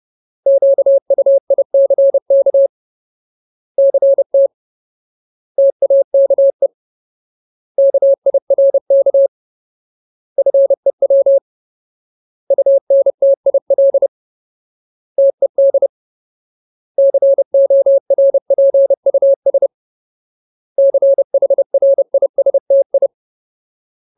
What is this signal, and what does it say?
Morse code, heard as a keyed tone, about 30 wpm, 550 Hz. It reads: QUICK CT TAKE KIRK FEW UNTIL TED CORPUS CHRISTI
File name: Ultimate Word List / 30WPM Ultimate